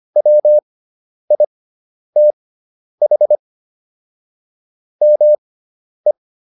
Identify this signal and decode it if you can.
Morse code, heard as a keyed tone, about 25 wpm, spaced as for 10 wpm, 600 Hz. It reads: WITH ME